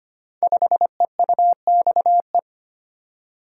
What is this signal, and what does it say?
Morse code, heard as a keyed tone, about 25 wpm, 700 Hz. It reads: 5EUXE
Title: Code Groups - 5 Letter:Numbers - 25wpm